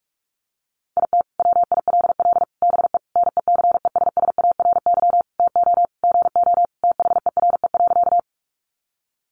Morse code, heard as a keyed tone, about 44 wpm, 700 Hz. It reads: IT WILL BE NECESSARY TO GO THERE.
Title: Sentences from Top 800 Words 45wpm